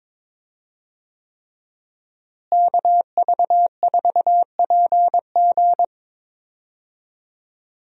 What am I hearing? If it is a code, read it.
Morse code, heard as a keyed tone, about 22 wpm, 700 Hz. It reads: KV4PG
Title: USA Call Signs - 5 Characters - Repeated 3x - 22wpm